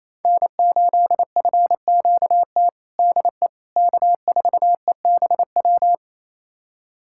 Morse code, heard as a keyed tone, about 28 wpm, 700 Hz. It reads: N8FQT DE K4EBW
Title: USA Callsign DE Callsign - 5 Characters - 28wpm